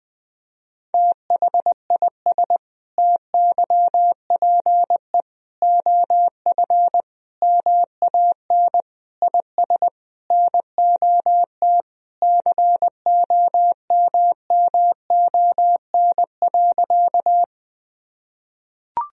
Morse code, heard as a keyed tone, about 20 wpm, 700 Hz. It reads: THIS TYPE OF MAN IS NOT COMMON.
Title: Sentences from Top 1000 Words - Review Part 3 - 20wpm